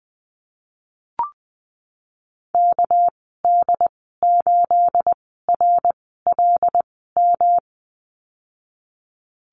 Morse code, heard as a keyed tone, about 20 wpm, spaced as for 15 wpm, 700 Hz. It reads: KD8RLM
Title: 231 Lesson 37g - Letter x - Call Signs 20wpm